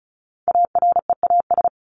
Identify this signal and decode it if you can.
Morse code, heard as a keyed tone, about 34 wpm, 700 Hz. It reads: AREAS
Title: ES:S Suffix - Top 100 Words with Base Words 35wpm